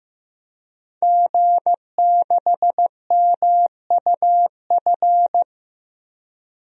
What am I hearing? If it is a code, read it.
Morse code, heard as a keyed tone, about 15 wpm, 700 Hz. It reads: G6MUF
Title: DXCC-100 Call Signs - 5 Characters - 15wpm